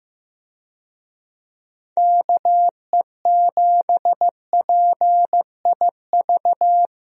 Morse code, heard as a keyed tone, about 15 wpm, 700 Hz. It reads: KE7PIV
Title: USA Call Signs - 6 Characters - Repeated 3x - 15wpm